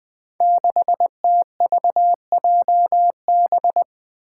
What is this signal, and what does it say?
Morse code, heard as a keyed tone, about 20 wpm, 700 Hz. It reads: 6TVJB
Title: Five Challenging Letter-Number Combinations 20wpm